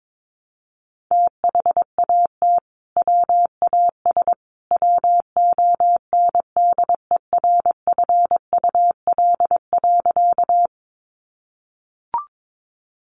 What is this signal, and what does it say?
Morse code, heard as a keyed tone, about 22 wpm, 700 Hz. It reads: THAT WAS WONDERFUL.
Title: Sentences from Top 1100 Words 22wpm